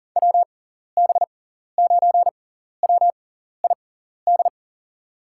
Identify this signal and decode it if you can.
Morse code, heard as a keyed tone, about 40 wpm, spaced as for 14 wpm, 700 Hz. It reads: WB9WID
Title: USA Call Signs - 6 Characters - 6x Character Spacing - 40wpm